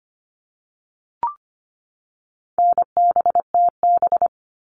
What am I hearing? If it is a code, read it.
Morse code, heard as a keyed tone, about 25 wpm, 700 Hz. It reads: NBTB
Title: All Cut Numbers - 4-Digit 25wpm